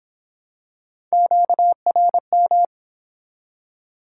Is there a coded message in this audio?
Yes: Morse code from 1.1 to 2.7 s, heard as a keyed tone, about 23 wpm, 700 Hz. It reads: QRM